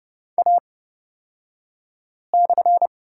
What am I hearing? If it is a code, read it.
Morse code, heard as a keyed tone, about 30 wpm, spaced as for 5 wpm, 700 Hz. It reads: A/
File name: Lesson 36d - BT - Two Characters 30wpm